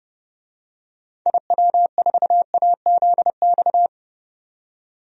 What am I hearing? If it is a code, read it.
Morse code, heard as a keyed tone, about 30 wpm, 700 Hz. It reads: IW4AZX
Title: DXCC-100 Call Signs - 6 Characters - 30wpm